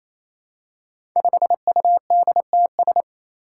Morse code, heard as a keyed tone, about 28 wpm, 700 Hz. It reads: 5UDTS
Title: Five Challenging Letter-Number Combinations 28wpm